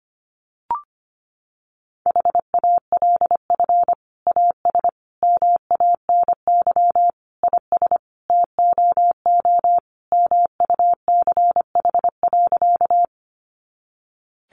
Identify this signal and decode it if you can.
Morse code, heard as a keyed tone, about 25 wpm, 700 Hz. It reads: HALF AS MANY IS TOO MUCH.
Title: Sentences from Top 400 Words 25wpm